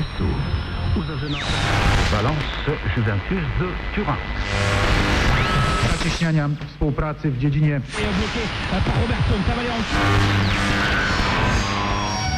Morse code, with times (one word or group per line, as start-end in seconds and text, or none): none